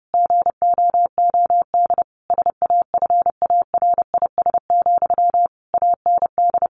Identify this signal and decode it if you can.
Morse code, heard as a keyed tone, about 30 wpm, 700 Hz. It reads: GOOD SAFARIS, AND
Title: A Sound Of Thunder 30wpm